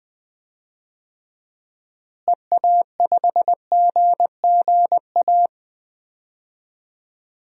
Morse code, heard as a keyed tone, about 20 wpm, 700 Hz. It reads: EA5GGA
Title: DXCC-100 Call Signs - 6 Characters - Repeated 3x - 20wpm